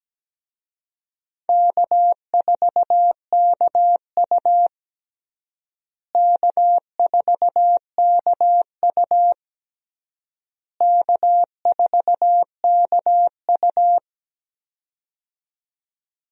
Morse code, heard as a keyed tone, about 17 wpm, 700 Hz. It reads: K4KU K4KU K4KU
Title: USA Call Signs - 4 Characters - Repeated 3x - 17wpm